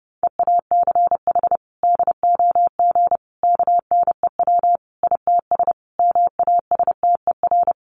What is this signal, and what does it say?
Morse code, heard as a keyed tone, about 30 wpm, 700 Hz. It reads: EACH DOG KNEW ITS MASTER
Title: Sentences from Top 1300 Words 30wpm